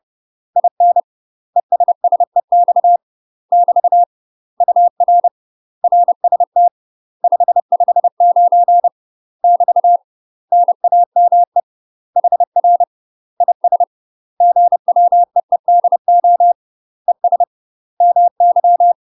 Morse code, heard as a keyed tone, about 30 wpm, 700 Hz. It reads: IN ESSEX <BT> UR RST 559 <BT> NAME HR IS GWEEDO ES MY
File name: UK MASTER QSOs TRAINING/30WPM